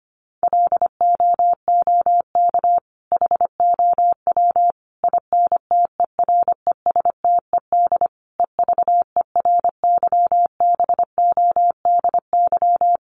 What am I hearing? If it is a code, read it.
Morse code, heard as a keyed tone, about 25 wpm, 700 Hz. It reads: LOOK HOW INTERESTED EVERYBODY
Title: Sentences from Top 1400 Words 25wpm